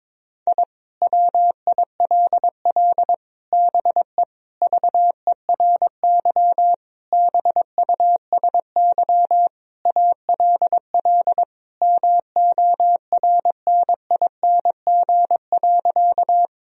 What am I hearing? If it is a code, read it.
Morse code, heard as a keyed tone, about 22 wpm, 700 Hz. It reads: I WILL BE VERY BUSY ALL MORNING.
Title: Sentences from Top 1000 Words - Review Part 3 - 22wpm